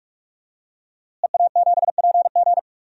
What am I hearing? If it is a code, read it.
Morse code, heard as a keyed tone, about 46 wpm, 700 Hz. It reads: EA7PG
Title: DXCC-100 Call Signs - 5 Characters - Repeated 3x - 45wpm